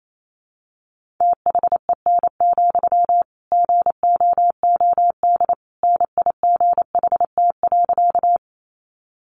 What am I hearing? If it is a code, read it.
Morse code, heard as a keyed tone, about 28 wpm, 700 Hz. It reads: THEN, GOOD NIGHT.
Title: Sentences from Top 300 Words 28wpm